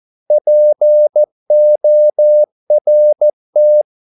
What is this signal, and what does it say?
Morse code, heard as a keyed tone, about 14 wpm, 600 Hz. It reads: PORT
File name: Korte ord DK 14wpm